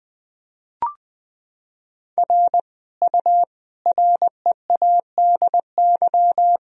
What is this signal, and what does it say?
Morse code, heard as a keyed tone, about 20 wpm, 700 Hz. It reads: R U READY
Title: Sentences from Top 400 Words and Abbreviations 20wpm